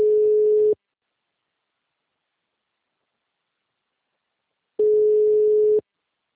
Ringback tone (Europe, 425 Hz).